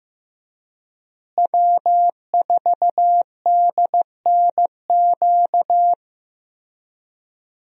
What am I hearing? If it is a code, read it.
Morse code, heard as a keyed tone, about 16 wpm, 700 Hz. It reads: W4DNQ
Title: USA Call Signs - 5 Characters - Repeated 3x - 15wpm